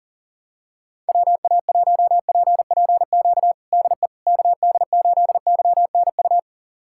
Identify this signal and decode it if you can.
Morse code, heard as a keyed tone, about 37 wpm, 700 Hz. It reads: WA1PPQ DE KD8YNU
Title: USA Callsign DE Callsign - 6 Characters - 40wpm